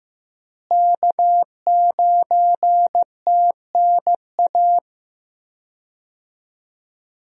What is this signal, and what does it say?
Morse code, heard as a keyed tone, about 15 wpm, 700 Hz. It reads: K9TNA
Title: USA Call Signs - 5 Characters - Repeated 3x - 15wpm